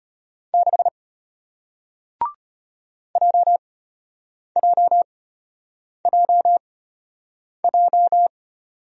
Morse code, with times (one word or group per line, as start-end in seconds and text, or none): 0.5-0.9 6
3.2-8.3 JJJJ